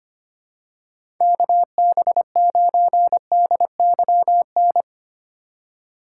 Morse code, heard as a keyed tone, about 25 wpm, 700 Hz. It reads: KB9DYN
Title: USA Call Signs - 6 Characters - 3x Character Spacing - 25wpm